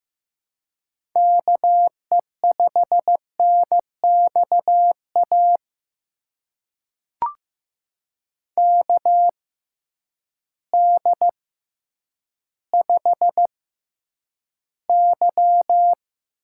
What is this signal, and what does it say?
Morse code, heard as a keyed tone, about 15 wpm, spaced as for 9 wpm, 700 Hz. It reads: KE5NXA KD5Y
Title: USA Call Signs - 6 Characters - 6x Character Spacing - 15wpm